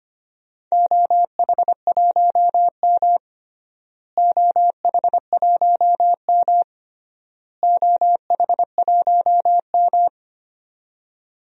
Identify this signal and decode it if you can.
Morse code, heard as a keyed tone, about 25 wpm, 700 Hz. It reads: OH1M OH1M OH1M
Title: DXCC-100 Call Signs - 4 Characters - Repeated 3x - 25wpm